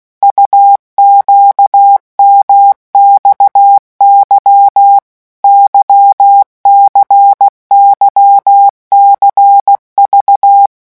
Morse code, heard as a keyed tone, about 16 wpm, 800 Hz. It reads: UQMXY YCYCV
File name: morse mp3 / revision2-vitesse 15 mots